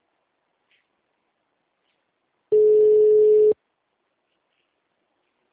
Ringback tone (Europe, 425 Hz).
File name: out- -2031- - - .2